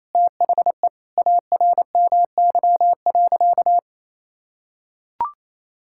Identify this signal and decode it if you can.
Morse code, heard as a keyed tone, about 28 wpm, 700 Hz. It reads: THE ARMY.
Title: Sentences from Top 1000 Words - Review Part 6 - 28wpm